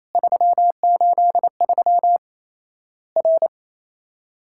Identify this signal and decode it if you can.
Morse code, heard as a keyed tone, about 28 wpm, 700 Hz. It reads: 383 R
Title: Worked All Europe DX Contest - QTC Exchange - 28wpm